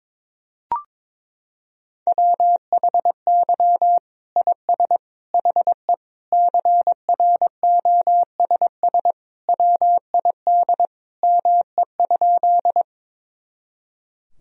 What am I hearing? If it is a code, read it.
Morse code, heard as a keyed tone, about 21 wpm, 700 Hz. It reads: WHY IS HE CROSS WID ME?
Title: Sentences from Top 300 Words and Abbreviations 22wpm